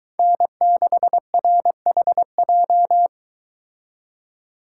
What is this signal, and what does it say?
Morse code, heard as a keyed tone, about 23 wpm, 700 Hz. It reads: N6RHJ